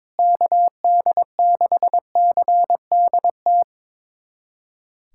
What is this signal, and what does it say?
Morse code, heard as a keyed tone, about 22 wpm, 700 Hz. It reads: KD6CDT
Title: USA Callsign DE Callsign 22wpm